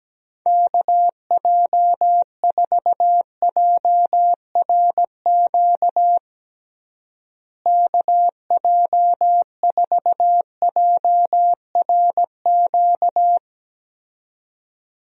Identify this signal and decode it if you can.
Morse code, heard as a keyed tone, about 17 wpm, 700 Hz. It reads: KJ4JRQ KJ4JRQ
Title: USA Call Signs - 6 Characters - Repeated 2x - 17wpm